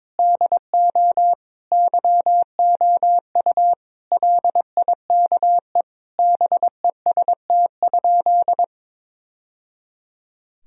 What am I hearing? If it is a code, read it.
Morse code, heard as a keyed tone, about 22 wpm, 700 Hz. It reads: DO YOU LIKE BEST?
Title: Sentences from Top 1000 Words - Review Part 1 - 22wpm